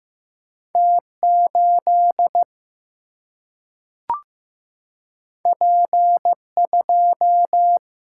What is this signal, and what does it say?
Morse code, heard as a keyed tone, about 15 wpm, 700 Hz. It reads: T 8 P 2
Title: Single-Letter Single-Number 15wpm